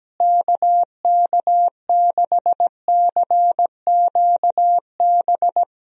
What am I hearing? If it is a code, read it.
Morse code, heard as a keyed tone, about 17 wpm, 700 Hz. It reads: KK6CQB
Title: USA Call Signs - 6 Characters - Repeated 2x - 17wpm